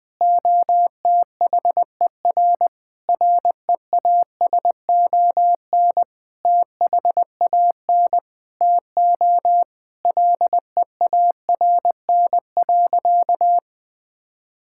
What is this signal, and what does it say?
Morse code, heard as a keyed tone, about 20 wpm, 700 Hz. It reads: OTHER REASON THAN TO LEARN.
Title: Sentences with 7-Word Phrases - 20wpm